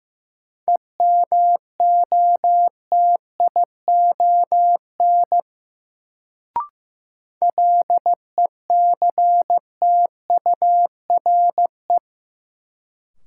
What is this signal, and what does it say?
Morse code, heard as a keyed tone, about 15 wpm, 700 Hz. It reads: EMOTION LECTURE
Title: Sentences from Top 1600 Words - New Word Drill - 15wpm